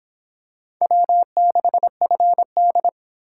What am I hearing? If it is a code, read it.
Morse code, heard as a keyed tone, about 25 wpm, 700 Hz. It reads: W6FD